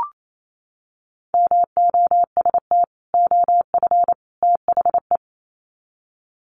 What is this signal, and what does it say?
Morse code, heard as a keyed tone, about 28 wpm, 700 Hz. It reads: MOST OF THE
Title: Sets of 3 Words 28wpm